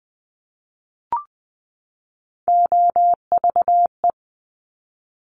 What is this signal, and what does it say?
Morse code, heard as a keyed tone, about 20 wpm, 700 Hz. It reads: OVE
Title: Top Three Letter Combinations in Words 20wpm